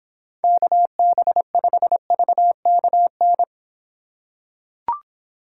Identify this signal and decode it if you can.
Morse code, heard as a keyed tone, about 26 wpm, 700 Hz. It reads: KB5VKN